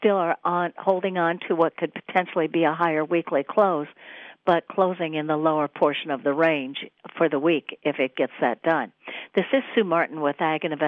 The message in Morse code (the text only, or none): none